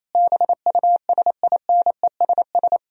Morse code, heard as a keyed tone, about 28 wpm, 700 Hz. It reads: BUSINESS
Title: Sentences from Top 1000 Words - Review Part 4 - 28wpm